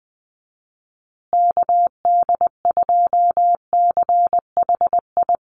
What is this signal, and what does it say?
Morse code, heard as a keyed tone, about 20 wpm, 700 Hz. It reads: KD2CHI
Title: USA Call Signs Repeated 2x 20wpm